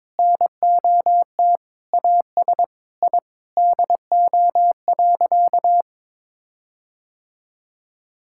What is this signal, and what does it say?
Morse code, heard as a keyed tone, about 22 wpm, 700 Hz. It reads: NOT AS I DO.
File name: practice / 22wpm / Sentences with 8-Word Phrases - 22wpm